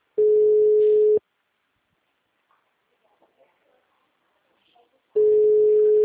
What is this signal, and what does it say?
Ringback tone (Europe, 425 Hz).
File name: out- -2018- - - .83534